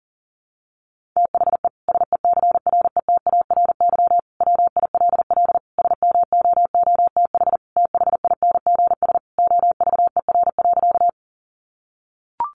Morse code, heard as a keyed tone, about 40 wpm, 700 Hz. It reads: THE SECRETARY WILL SMOOTH THINGS OVER.
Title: Sentences from Top 1400 Words 40wpm